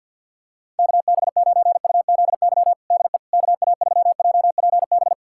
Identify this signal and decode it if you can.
Morse code, heard as a keyed tone, about 51 wpm, 700 Hz. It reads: KB9UZY DE KA3JPB
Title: USA Callsign DE Callsign - 6 Characters - 50wpm